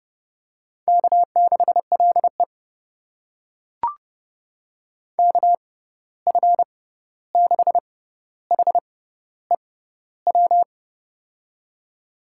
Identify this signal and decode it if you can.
Morse code, heard as a keyed tone, about 30 wpm, spaced as for 14 wpm, 700 Hz. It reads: K6LE KF6HEW
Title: USA Call Signs - 6x Character Spacing - 30wpm